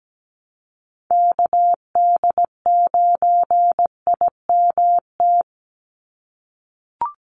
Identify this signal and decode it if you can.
Morse code, heard as a keyed tone, about 17 wpm, 700 Hz. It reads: KD9IMT
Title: USA Call Signs 17wpm